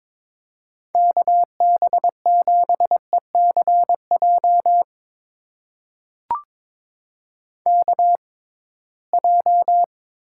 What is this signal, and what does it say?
Morse code, heard as a keyed tone, about 22 wpm, spaced as for 17 wpm, 700 Hz. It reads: KB7ECJ KJ
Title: USA Call Signs - 6 Characters - 6x Character Spacing - 22wpm